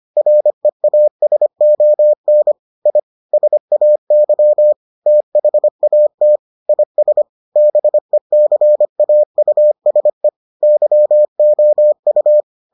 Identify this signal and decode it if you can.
Morse code, heard as a keyed tone, about 25 wpm, 600 Hz. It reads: REASON I SAY THAT IS BECAUSE YOU